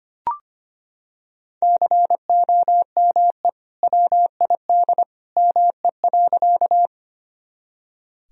Morse code, heard as a keyed tone, about 25 wpm, 700 Hz. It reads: COME WID ME.
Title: Sentences from Top 200 Words and Abbreviations 25wpm